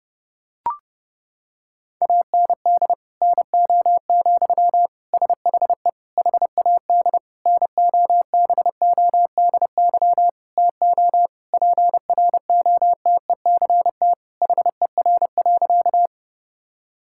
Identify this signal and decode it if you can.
Morse code, heard as a keyed tone, about 30 wpm, 700 Hz. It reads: AND NO, SHE HAD NOBODY TO PROTECT HER.
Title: Sentences from Top 1600 Words 30wpm